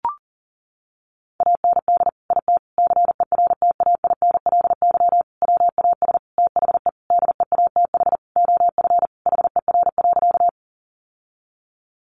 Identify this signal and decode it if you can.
Morse code, heard as a keyed tone, about 40 wpm, 700 Hz. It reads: AND IT CERTAINLY WAS THE DEATH OF HER.
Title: Sentences from Top 1100 Words 40wpm